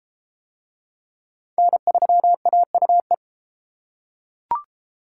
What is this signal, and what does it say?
Morse code, heard as a keyed tone, about 29 wpm, 700 Hz. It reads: N3AUE